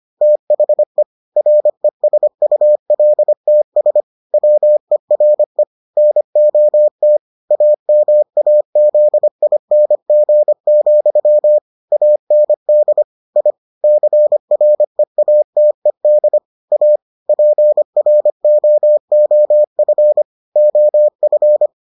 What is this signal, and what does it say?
Morse code, heard as a keyed tone, about 25 wpm, 600 Hz. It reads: THE RESULTS WERE NOT AMAZING, AND I CREATED A PROOF OF